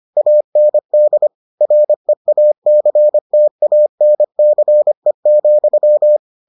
Morse code, heard as a keyed tone, about 25 wpm, 600 Hz. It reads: AND REACTANCE,